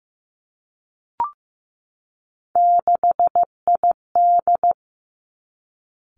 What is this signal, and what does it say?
Morse code, heard as a keyed tone, about 15 wpm, 700 Hz. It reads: 6ID
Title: Three Challenging Letter-Number Combinations 15wpm